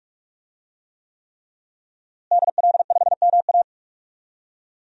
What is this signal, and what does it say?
Morse code, heard as a keyed tone, about 46 wpm, 700 Hz. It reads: DL5MA
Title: DXCC-100 Call Signs - 5 Characters - Repeated 3x - 45wpm